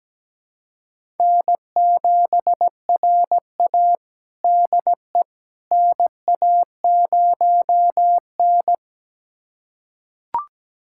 Morse code, heard as a keyed tone, about 17 wpm, 700 Hz. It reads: N7RA DE NA0N
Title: USA Callsign DE Callsign - 4 Characters - 17wpm